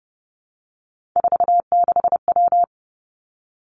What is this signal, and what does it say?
Morse code, heard as a keyed tone, about 30 wpm, 700 Hz. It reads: 46W
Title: Three Challenging Letter-Number Combinations 30wpm